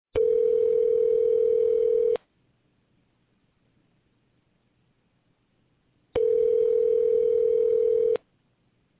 Ringback tone (North America), 440 + 480 Hz.